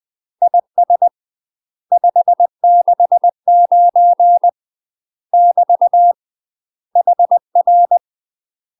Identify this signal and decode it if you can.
Morse code, heard as a keyed tone, about 20 wpm, 700 Hz. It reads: IS 569 <BT> HR
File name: UK MASTER QSOs TRAINING / 20WPM / UK MASTER QSOs 0055